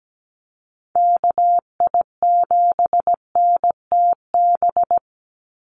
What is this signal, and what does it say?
Morse code, heard as a keyed tone, about 17 wpm, 700 Hz. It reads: KI7NTB
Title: USA Call Signs 17wpm